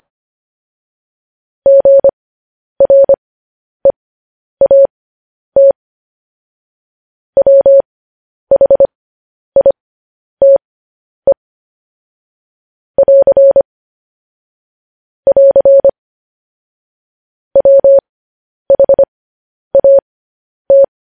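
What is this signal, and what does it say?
Morse code, heard as a keyed tone, about 25 wpm, spaced as for 10 wpm, 550 Hz. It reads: GREAT WHITE <AR> <AR> WHAT
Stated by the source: Morse code transcription of news headlines sent at 10 wpm.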